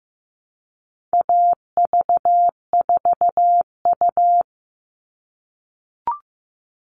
Morse code, heard as a keyed tone, about 15 wpm, 700 Hz. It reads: AV4U
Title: All Cut Numbers - 4-Digit 15wpm